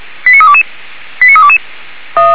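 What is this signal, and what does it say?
Five-tone selective call (ZVEI), heard as quick runs of five tones, tones of about 70 ms. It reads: 89322 89322
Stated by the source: ZVEI five-tone selective call received on an SDR by fire brigade Orlamünde (feuerwehr-orlamuende).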